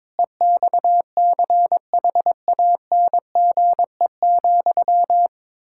Morse code, heard as a keyed tone, about 22 wpm, 700 Hz. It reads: EXCHANGE,
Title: Sentences from Top 1000 Words - Review Part 6 - 22wpm